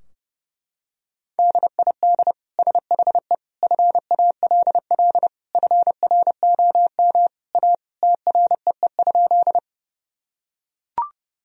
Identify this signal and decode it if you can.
Morse code, heard as a keyed tone, about 30 wpm, 700 Hz. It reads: DID SHE FALL FROM A TREE?
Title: Sentences from Top 1000 Words - Review Part 2 - 30wpm